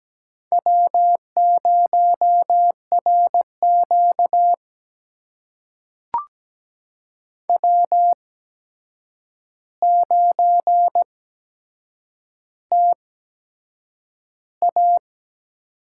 Morse code, heard as a keyed tone, about 17 wpm, spaced as for 7 wpm, 700 Hz. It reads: W0RQ W9TA
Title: USA Call Signs - 8x Character Spacing - 17wpm